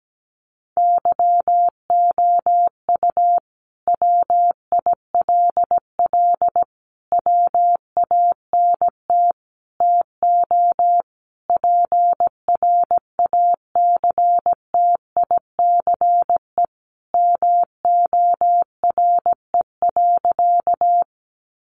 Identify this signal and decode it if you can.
Morse code, heard as a keyed tone, about 17 wpm, 700 Hz. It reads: YOU WILL WANT TO PRACTICE MORE.
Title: Sentences from Top 700 Words 17wpm